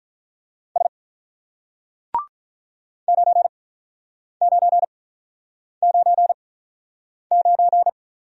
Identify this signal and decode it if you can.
Morse code, heard as a keyed tone, about 47 wpm, spaced as for 9 wpm, 700 Hz. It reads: S 9999